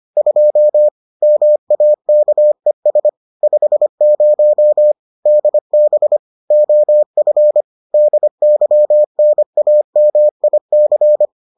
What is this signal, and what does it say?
Morse code, heard as a keyed tone, about 25 wpm, 600 Hz. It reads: 2 MAKES 50 DB OF DYNAMIC